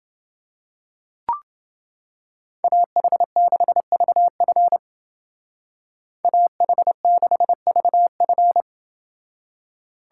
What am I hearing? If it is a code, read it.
Morse code, heard as a keyed tone, about 28 wpm, 700 Hz. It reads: AH6VF AH6VF